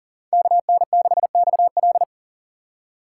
Morse code, heard as a keyed tone, about 38 wpm, 700 Hz. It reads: KN6XL
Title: USA Call Signs - 5 Characters - 4x Character Spacing - 40wpm